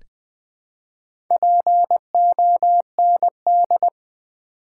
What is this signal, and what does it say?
Morse code, heard as a keyed tone, about 20 wpm, 700 Hz. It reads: POND
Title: Build A Word - 1 Letter At A Time - 3 to 5-Letter Words - 20wpm